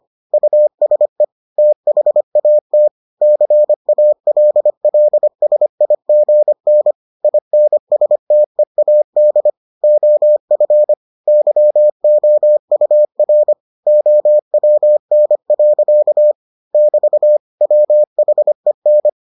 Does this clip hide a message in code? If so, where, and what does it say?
Yes: Morse code from 0.3 to 19.1 s, heard as a keyed tone, about 25 wpm, 600 Hz. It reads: USE THAT CALLSIGN INSTEAD OF YOUR OWN. <BT> WHEN